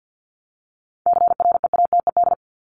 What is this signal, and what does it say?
Morse code, heard as a keyed tone, about 50 wpm, 700 Hz. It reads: CREATED
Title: ED Suffix - Top 100 Words with Base Words 50wpm